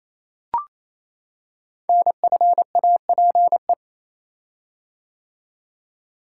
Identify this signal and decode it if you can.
Morse code, heard as a keyed tone, about 27 wpm, 700 Hz. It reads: NFAPE
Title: Code Groups - 5 Letters - 28wpm